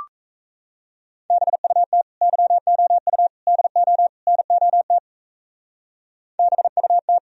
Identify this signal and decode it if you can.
Morse code, heard as a keyed tone, about 35 wpm, 700 Hz. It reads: BUT YOU DO NOT BUT
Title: Speed-Racing - Sets of 4 Words 28wpm